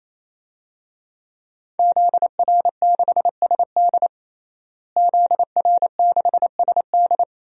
Morse code, heard as a keyed tone, about 28 wpm, 700 Hz. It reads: ZR6SD ZR6SD
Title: DXCC-100 Call Signs - 5 Characters - Repeated 3x - 28wpm